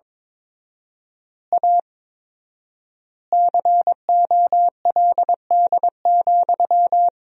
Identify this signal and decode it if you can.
Morse code, heard as a keyed tone, about 22 wpm, 700 Hz. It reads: A COLD,
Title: 4x Word-Spacing - Sentences from Top 500 Words - 22wpm